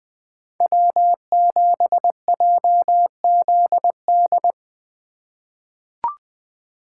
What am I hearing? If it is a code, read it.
Morse code, heard as a keyed tone, about 21 wpm, 700 Hz. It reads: W7JZD
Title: USA Call Signs - 5 Characters - 4x Character Spacing - 20wpm